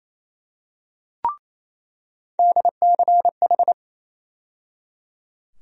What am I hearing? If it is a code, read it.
Morse code, heard as a keyed tone, about 28 wpm, 700 Hz. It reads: DCH
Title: USA Call Sign Postfixes - 3 Letters 28wpm